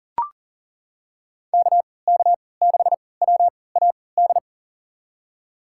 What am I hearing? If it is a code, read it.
Morse code, heard as a keyed tone, about 40 wpm, spaced as for 23 wpm, 700 Hz. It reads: KK6WAD
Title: USA Call Signs - 6 Characters - 3x Character Spacing - 40wpm